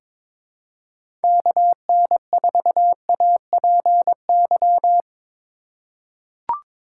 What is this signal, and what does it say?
Morse code, heard as a keyed tone, about 22 wpm, 700 Hz. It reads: KN4APY
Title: USA Call Signs - 2x Character Spacing - 22wpm